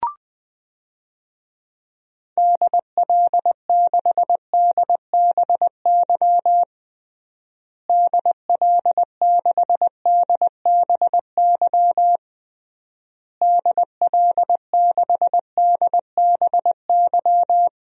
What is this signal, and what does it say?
Morse code, heard as a keyed tone, about 20 wpm, 700 Hz. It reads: DL6DBY DL6DBY DL6DBY
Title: DXCC-100 Call Signs - 6 Characters - Repeated 3x - 20wpm